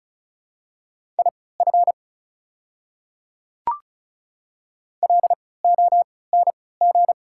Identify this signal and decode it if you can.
Morse code, heard as a keyed tone, about 34 wpm, spaced as for 20 wpm, 690 Hz. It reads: IF LONG
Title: 3x Character Spacing - Top 100 Words - 35wpm